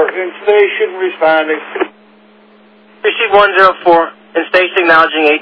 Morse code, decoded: none

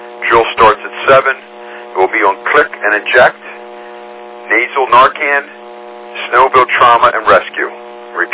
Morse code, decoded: none

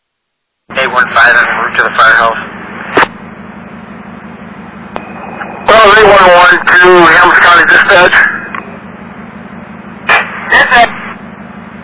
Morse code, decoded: none

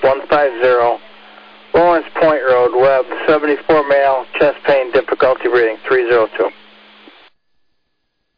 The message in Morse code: none

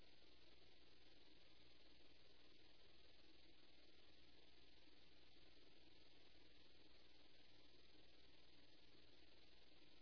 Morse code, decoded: none

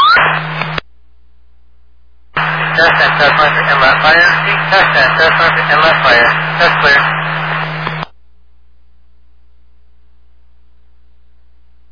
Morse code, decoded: none

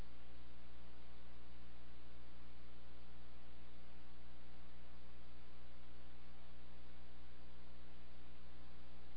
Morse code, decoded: none

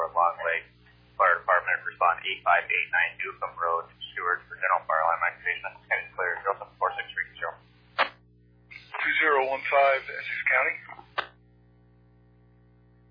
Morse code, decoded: none

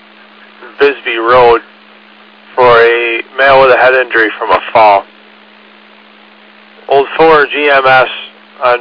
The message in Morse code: none

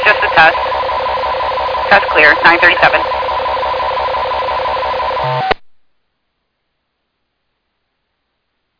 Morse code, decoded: none

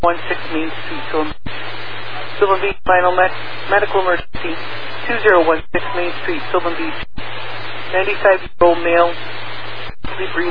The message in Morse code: none

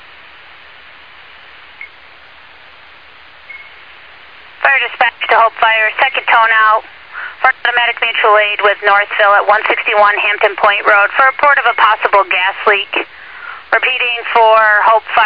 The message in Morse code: none

